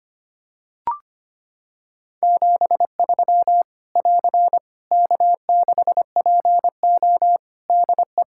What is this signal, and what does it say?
Morse code, heard as a keyed tone, about 25 wpm, 700 Hz. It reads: 73 <AR> K6PO DE
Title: K1USN SST QSO Practice 25wpm